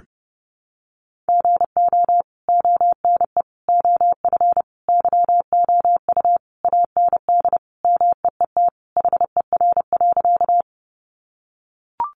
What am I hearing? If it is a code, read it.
Morse code, heard as a keyed tone, about 30 wpm, 700 Hz. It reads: GO ONE OF YOU AND MEET HER.
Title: Sentences from Top 1100 Words 30wpm